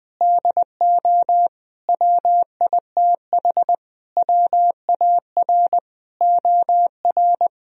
Morse code, heard as a keyed tone, about 20 wpm, 700 Hz. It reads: DO WITH WAR OR
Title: Sentences from Top 1000 Words - Review Part 3 - 20wpm